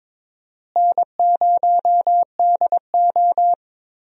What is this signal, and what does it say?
Morse code, heard as a keyed tone, about 22 wpm, 700 Hz. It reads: N0DO
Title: USA Call Signs - 4 Characters - Repeated 2x - 22wpm